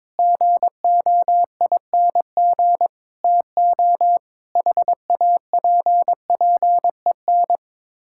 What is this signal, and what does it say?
Morse code, heard as a keyed tone, about 22 wpm, 700 Hz. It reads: GOING TO HAPPEN
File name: practice / 22wpm / Sets of 7 Words - 22wpm